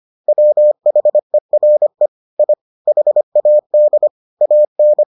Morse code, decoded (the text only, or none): WHERE I HAD AN